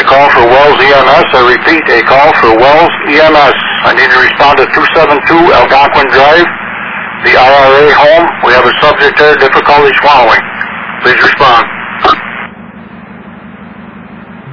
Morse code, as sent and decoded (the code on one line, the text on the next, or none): none